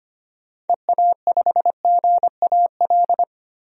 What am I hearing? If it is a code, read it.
Morse code, heard as a keyed tone, about 25 wpm, 700 Hz. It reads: EA5GAL
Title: DXCC-100 Call Signs - 6 Characters - 25wpm